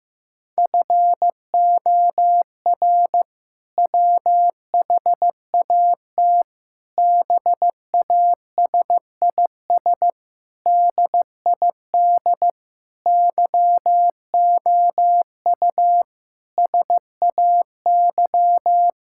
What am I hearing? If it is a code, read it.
Morse code, heard as a keyed tone, about 15 wpm, 700 Hz. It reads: FOR WHAT BASIS DID YOU SAY
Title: Sentences from Top 1000 Words - Review Part 5 - 15wpm